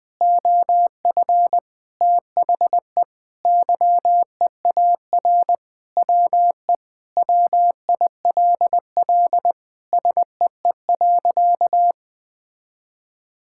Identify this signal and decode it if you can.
Morse code, heard as a keyed tone, about 20 wpm, 700 Hz. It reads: OF THE YEAR WE WILL SEE.
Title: Sentences with 3-Word Phrases - 20wpm